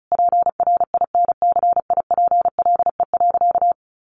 Morse code, heard as a keyed tone, about 34 wpm, 700 Hz. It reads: PRINCIPLE.
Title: Sentences from Top 1400 Words 35wpm